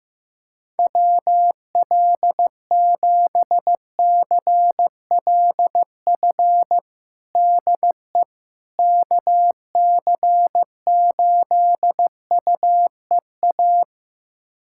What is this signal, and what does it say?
Morse code, heard as a keyed tone, about 15 wpm, 700 Hz. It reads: WL7CLF DE KC8UEA
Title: USA Callsign DE Callsign - 6 Characters - 15wpm